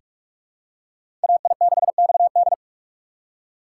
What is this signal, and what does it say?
Morse code, heard as a keyed tone, about 42 wpm, 700 Hz. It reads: AI6XD
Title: USA Call Signs - 5 Characters - 4x Character Spacing - 45wpm